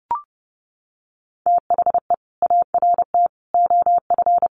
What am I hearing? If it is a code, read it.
Morse code, heard as a keyed tone, about 30 wpm, 700 Hz. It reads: THE ART OF
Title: Sentences from Top 1300 Words 30wpm